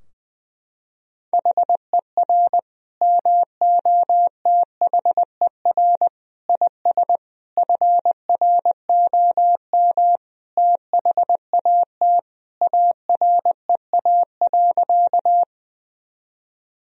Morse code, heard as a keyed tone, about 20 wpm, 700 Hz. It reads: HER MOTHER IS FROM THAT AREA.
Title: Sentences from Top 1000 Words - Review Part 2 - 20wpm